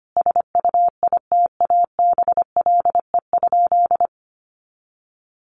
Morse code, heard as a keyed tone, about 25 wpm, 700 Hz. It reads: SUITABLE?
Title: Sentences from Top 1400 Words 25wpm